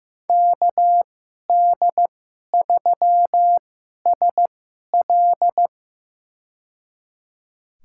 Morse code, heard as a keyed tone, about 16 wpm, spaced as for 11 wpm, 700 Hz. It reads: KD3SL
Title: USA Call Signs - 5 Characters - 2x Character Spacing - 15wpm